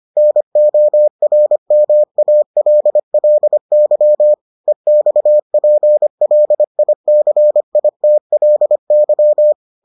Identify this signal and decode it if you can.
Morse code, heard as a keyed tone, about 25 wpm, 600 Hz. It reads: NORMALLY EXPLICITLY